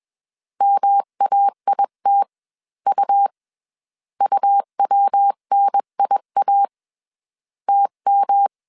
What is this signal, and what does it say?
Morse code, heard as a keyed tone, about 21 wpm, spaced as for 18 wpm, 800 Hz. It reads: MAITU UWNIA TM
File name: CW / DET / 1 3 1500C S B 20wpm 1754